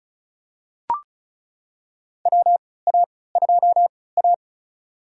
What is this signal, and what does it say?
Morse code, heard as a keyed tone, about 36 wpm, spaced as for 20 wpm, 700 Hz. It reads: WA2A
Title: USA Call Signs - 3x Character Spacing - 35wpm